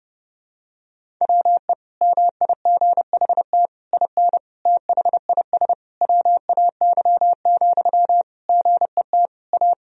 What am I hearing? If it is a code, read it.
Morse code, heard as a keyed tone, about 32 wpm, 700 Hz. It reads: WE MIGHT IN THIS WAY, GET A
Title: Sentences from Top 1000 Words - Review Part 1 - 30wpm